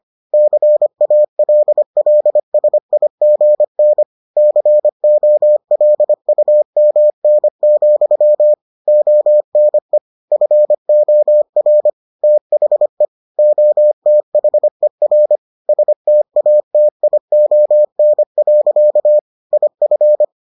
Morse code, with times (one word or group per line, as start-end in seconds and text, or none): 0.3-4.0 CALLSIGN
4.4-8.5 COLUMN,
8.9-10.0 ONE
10.3-11.9 FOR
12.2-13.1 THE
13.4-15.4 OTHER
15.7-19.2 STATION.
19.5-20.2 IF